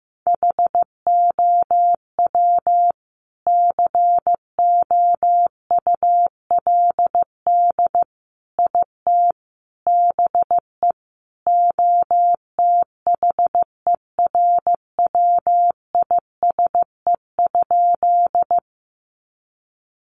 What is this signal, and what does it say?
Morse code, heard as a keyed tone, about 15 wpm, 700 Hz. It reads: HOW COULD IT BE OTHERWISE?
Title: Sentences from Top 1200 Words 15wpm